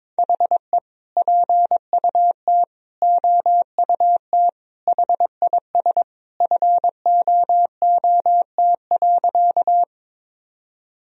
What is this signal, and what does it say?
Morse code, heard as a keyed tone, about 22 wpm, 700 Hz. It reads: HE PUT OUT HIS FOOT.
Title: Sentences from Top 1000 Words - Review Part 3 - 22wpm